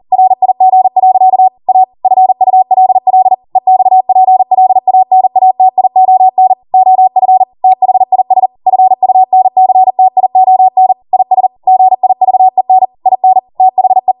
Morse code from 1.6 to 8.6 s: A FULL EXPLANATION OF THIS